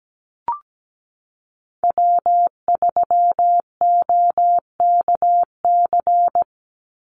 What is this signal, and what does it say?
Morse code, heard as a keyed tone, about 17 wpm, 700 Hz. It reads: W3OKC
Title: Call Signs Repeated 3x - 17wpm